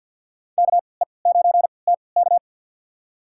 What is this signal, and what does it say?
Morse code, heard as a keyed tone, about 52 wpm, spaced as for 29 wpm, 700 Hz. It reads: KE9TK
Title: USA Call Signs - 5 Characters - 3x Character Spacing - 50wpm